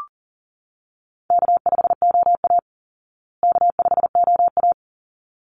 Morse code, heard as a keyed tone, about 40 wpm, 700 Hz. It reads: K5OA K5OA
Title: Easy Call Signs Repeated 2x - 40wpm